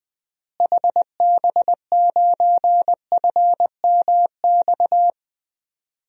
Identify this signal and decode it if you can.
Morse code, heard as a keyed tone, about 20 wpm, 700 Hz. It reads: HB9FMX
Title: DXCC-100 Call Signs - 6 Characters - Repeated 2x - 20wpm